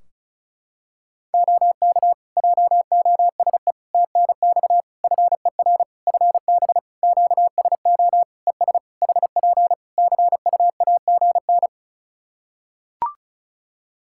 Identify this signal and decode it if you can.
Morse code, heard as a keyed tone, about 34 wpm, 700 Hz. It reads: OK JOSE TNX FER FB QSO ES HP CUAGN
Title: K1USN SST QSO Practice 35wpm